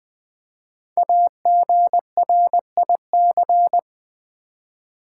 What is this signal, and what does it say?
Morse code, heard as a keyed tone, about 20 wpm, 700 Hz. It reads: AGRIC
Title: Build A Word - 5 Letters At A Time - 12-Letter Words - 20wpm